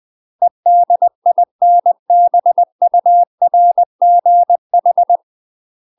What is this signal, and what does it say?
Morse code, heard as a keyed tone, about 20 wpm, 700 Hz. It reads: EDINBURGH
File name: UK MASTER QSOs TRAINING / 20WPM / UK MASTER QSOs 0074